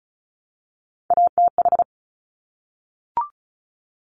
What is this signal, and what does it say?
Morse code, heard as a keyed tone, about 36 wpm, 700 Hz. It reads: ATH
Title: Top Three Letter Combinations in Words 35wpm